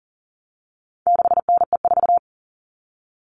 Morse code, heard as a keyed tone, about 40 wpm, 700 Hz. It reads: 6NE4
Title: All Cut Numbers - 4-Digit 40wpm